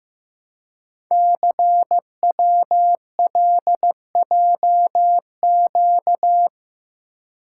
Morse code, heard as a keyed tone, about 15 wpm, 700 Hz. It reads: CWLJQ
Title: Code Groups - 5 Letters - 15wpm